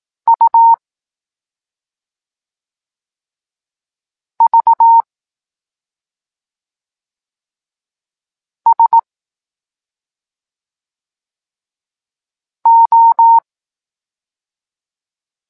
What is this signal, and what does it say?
Morse code, heard as a keyed tone, about 18 wpm, 920 Hz. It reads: U V S O
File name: Tekst-9-rownomierny